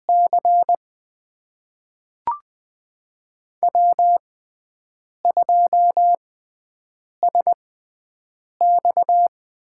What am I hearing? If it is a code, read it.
Morse code, heard as a keyed tone, about 20 wpm, spaced as for 7 wpm, 700 Hz. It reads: C W2SX
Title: USA Call Signs - 4 Characters - 6x Character Spacing - 20wpm